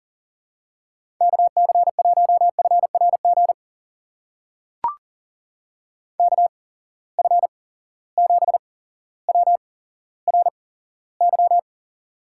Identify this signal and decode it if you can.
Morse code, heard as a keyed tone, about 38 wpm, spaced as for 17 wpm, 700 Hz. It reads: KC1FRG KF7WRY
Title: USA Call Signs - 6 Characters - 8x Character Spacing - 40wpm